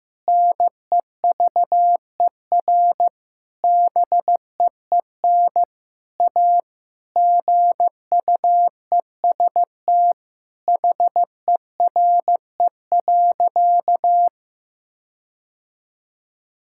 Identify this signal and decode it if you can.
Morse code, heard as a keyed tone, about 15 wpm, 700 Hz. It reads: NEVER BEEN A GUEST HERE.
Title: Sentences from Top 1600 Words 15wpm